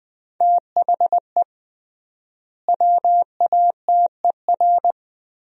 Morse code, heard as a keyed tone, about 20 wpm, 700 Hz. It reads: THE WATER
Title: 3x Word-Spacing - Sentences from Top 500 Words - 20wpm